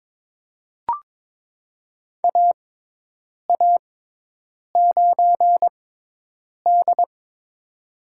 Morse code, heard as a keyed tone, about 21 wpm, spaced as for 8 wpm, 700 Hz. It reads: AA9D
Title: USA Call Signs - 4 Characters - 6x Character Spacing - 22wpm